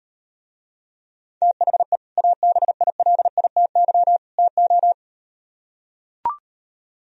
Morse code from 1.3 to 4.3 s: THE ABILITY

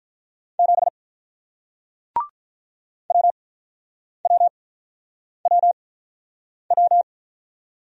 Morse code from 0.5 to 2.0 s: 7

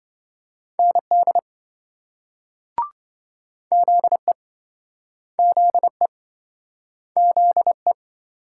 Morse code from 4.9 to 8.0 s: ZE ZE

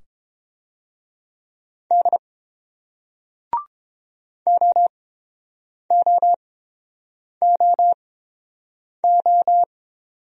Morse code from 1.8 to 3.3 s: D

